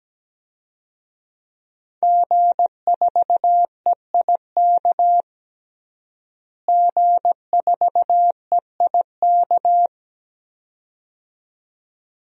Morse code, heard as a keyed tone, about 17 wpm, 700 Hz. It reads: G4EIK G4EIK
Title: DXCC-100 Call Signs - 5 Characters - Repeated 2x - 17wpm